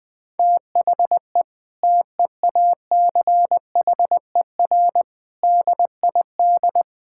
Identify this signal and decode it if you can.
Morse code, heard as a keyed tone, about 20 wpm, 700 Hz. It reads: THE TEACHER DID
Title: Sentences from Top 1000 Words - Review Part 6 - 20wpm